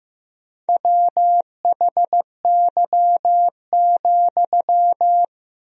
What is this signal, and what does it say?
Morse code, heard as a keyed tone, about 15 wpm, 700 Hz. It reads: WHY,
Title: 3x Word-Spacing - Sentences from Top 200 Words - 15wpm